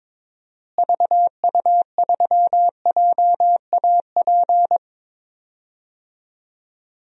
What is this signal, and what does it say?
Morse code, heard as a keyed tone, about 22 wpm, 700 Hz. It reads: VU3JAP
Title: DXCC-100 Call Signs - 6 Characters - Repeated 3x - 22wpm